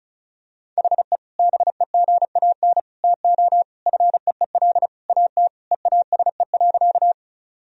Morse code, heard as a keyed tone, about 34 wpm, 700 Hz. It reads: HE BEGAN TO FEEL AT EASE.
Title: Sentences from Top 1000 Words - Review Part 1 - 35wpm